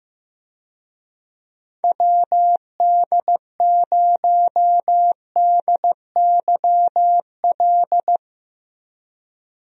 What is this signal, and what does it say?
Morse code, heard as a keyed tone, about 16 wpm, 700 Hz. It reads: WD0DYL
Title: USA Call Signs - 6 Characters - Repeated 2x - 15wpm